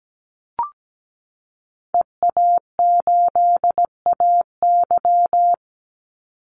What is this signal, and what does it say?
Morse code, heard as a keyed tone, about 17 wpm, 700 Hz. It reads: EA8AY
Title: Call Signs - 17wpm